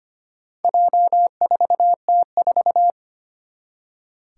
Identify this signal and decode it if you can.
Morse code, heard as a keyed tone, about 25 wpm, 700 Hz. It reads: J4T4
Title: Four Challenging Letter-Number Combinations 25wpm